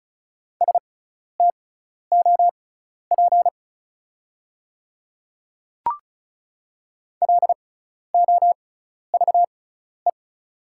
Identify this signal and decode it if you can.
Morse code, heard as a keyed tone, about 34 wpm, spaced as for 12 wpm, 690 Hz. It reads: STOP LOVE
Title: 6x Character Spacing - Top 500 Words - 35wpm